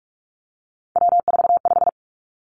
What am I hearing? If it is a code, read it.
Morse code, heard as a keyed tone, about 46 wpm, 700 Hz. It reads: W45
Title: Three Challenging Letter-Number Combinations 45wpm